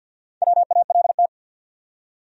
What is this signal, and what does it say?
Morse code, heard as a keyed tone, about 52 wpm, 700 Hz. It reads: WALT
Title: USA Common Names - Max 5 Letters - 50wpm